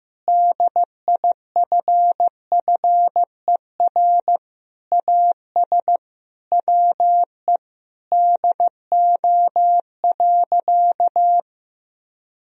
Morse code, heard as a keyed tone, about 15 wpm, 700 Hz. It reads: DIFFER AS WE DO.
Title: Sentences from Top 1000 Words - Review Part 1 - 15wpm